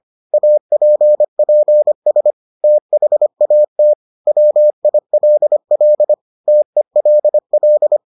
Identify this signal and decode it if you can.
Morse code, heard as a keyed tone, about 25 wpm, 600 Hz. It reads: APPS THAT WILL TELL